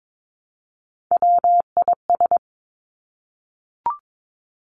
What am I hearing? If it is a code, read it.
Morse code, heard as a keyed tone, about 22 wpm, 700 Hz. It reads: WIS
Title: Three Challenging Letter-Number Combinations 22wpm